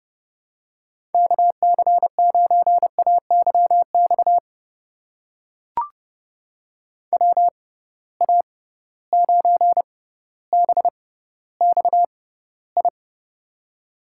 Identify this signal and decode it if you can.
Morse code, heard as a keyed tone, about 32 wpm, spaced as for 16 wpm, 700 Hz. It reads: KC9AYX WA9BXI
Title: USA Call Signs - 6 Characters - 6x Character Spacing - 30wpm